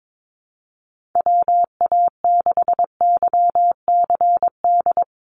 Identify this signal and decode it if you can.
Morse code, heard as a keyed tone, about 22 wpm, 700 Hz. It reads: WA6YCD
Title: USA Call Signs Repeated 2x 22wpm